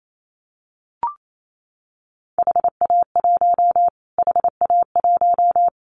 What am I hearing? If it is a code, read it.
Morse code, heard as a keyed tone, about 28 wpm, 700 Hz. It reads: HA1 HA1
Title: Callsign Prefixes for Top 25 DXCC - Repeated 3x - 28wpm